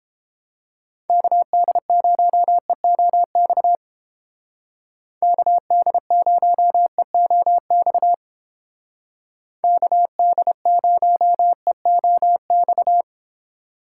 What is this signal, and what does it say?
Morse code, heard as a keyed tone, about 30 wpm, 700 Hz. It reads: KD0EOX KD0EOX KD0EOX